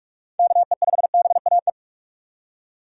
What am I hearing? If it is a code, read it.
Morse code, heard as a keyed tone, about 46 wpm, 700 Hz. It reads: KE5BAE
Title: USA Call Signs - 6 Characters - 45wpm